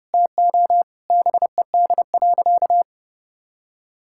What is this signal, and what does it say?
Morse code, heard as a keyed tone, about 30 wpm, 700 Hz. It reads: TO BED.
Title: Sentences from Top 1000 Words - Review Part 6 - 30wpm